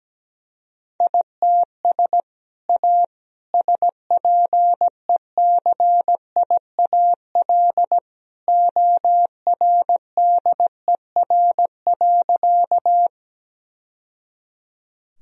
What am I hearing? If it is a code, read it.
Morse code, heard as a keyed tone, about 17 wpm, 700 Hz. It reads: ITS A SPECIAL ORDER.
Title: Sentences from Top 1000 Words - Review Part 3 - 17wpm